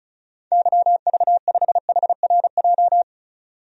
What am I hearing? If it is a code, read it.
Morse code, heard as a keyed tone, about 36 wpm, 700 Hz. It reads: YV5HRJ
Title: DXCC-100 Call Signs - 6 Characters - 35wpm